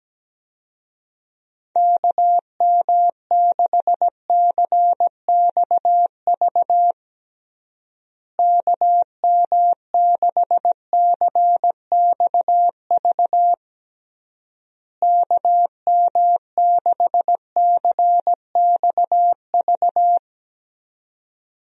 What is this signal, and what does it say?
Morse code, heard as a keyed tone, about 17 wpm, 700 Hz. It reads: KM6CXV KM6CXV KM6CXV
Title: USA Call Signs - 6 Characters - Repeated 3x - 17wpm